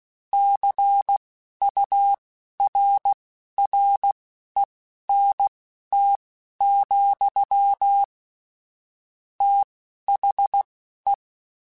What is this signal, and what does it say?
Morse code, heard as a keyed tone, about 16 wpm, spaced as for 11 wpm, 800 Hz. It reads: CURRENT, THE
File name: Technical Subjects / Technical Subjects-mp3s / XL @10